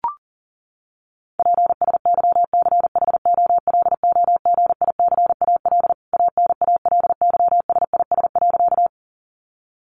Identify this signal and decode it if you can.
Morse code, heard as a keyed tone, about 40 wpm, 700 Hz. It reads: PSYCHOLOGICAL ANALYSIS.
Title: Sentences from Top 1400 Words 40wpm